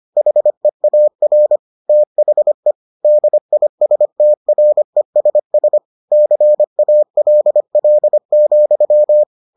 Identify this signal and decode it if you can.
Morse code, heard as a keyed tone, about 25 wpm, 600 Hz. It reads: HEAR THE DISTRESS CALL,